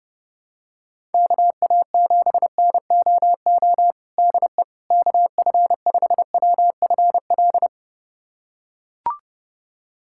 Morse code, heard as a keyed tone, about 30 wpm, 700 Hz. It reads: KA7NOO DE KF5WFL
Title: USA Callsign DE Callsign 30wpm